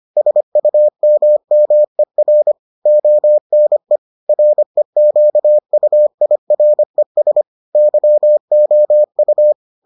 Morse code, heard as a keyed tone, about 25 wpm, 600 Hz. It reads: SUMMER ONE REQUIRES YOU